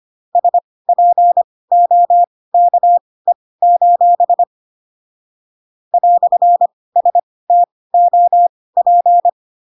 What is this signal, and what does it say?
Morse code, heard as a keyed tone, about 25 wpm, spaced as for 18 wpm, 700 Hz. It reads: SPOKE: "STOP
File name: Pinocchio / Pinocchio ch1 12 25 ik1hge